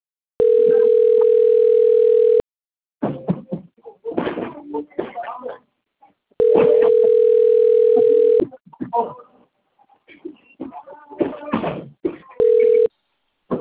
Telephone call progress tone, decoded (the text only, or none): ringback tone